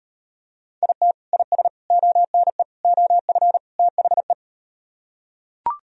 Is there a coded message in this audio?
Yes: Morse code from 0.8 to 4.3 s, heard as a keyed tone, about 38 wpm, 700 Hz. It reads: IT IS ONE OF THE